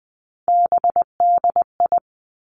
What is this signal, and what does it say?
Morse code, heard as a keyed tone, about 20 wpm, 700 Hz. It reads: BDI
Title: Three Challenging Letter-Number Combinations 20wpm